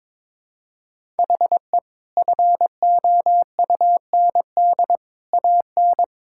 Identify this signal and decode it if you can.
Morse code, heard as a keyed tone, about 21 wpm, 700 Hz. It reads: HE FOUND AN
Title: Sentences from Top 1000 Words - Review Part 6 - 22wpm